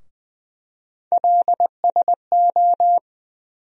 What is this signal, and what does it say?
Morse code, heard as a keyed tone, about 20 wpm, 700 Hz. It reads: LSO
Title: USA Call Sign Postfixes - 3 Letters 20wpm